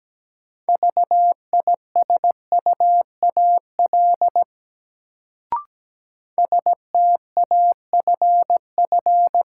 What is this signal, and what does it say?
Morse code, heard as a keyed tone, about 17 wpm, 700 Hz. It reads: VISUAL STAFF
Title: Sets of 5 Words Encore - New Word Drill - 17wpm